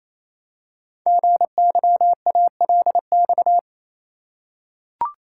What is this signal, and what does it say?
Morse code, heard as a keyed tone, about 27 wpm, 700 Hz. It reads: GYALX
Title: Code Groups - 5 Letters - 28wpm